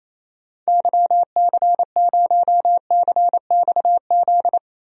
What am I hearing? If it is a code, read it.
Morse code, heard as a keyed tone, about 28 wpm, 700 Hz. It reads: YC0CXZ
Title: DXCC-100 Call Signs - 6 Characters - Repeated 2x - 28wpm